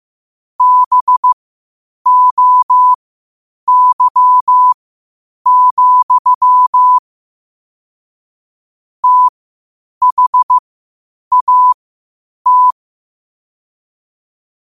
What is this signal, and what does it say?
Morse code, heard as a keyed tone, about 15 wpm, spaced as for 8 wpm, 1000 Hz. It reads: BOY, THAT
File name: qotdfiles / 8wpm